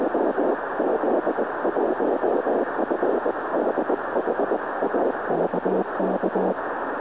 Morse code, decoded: none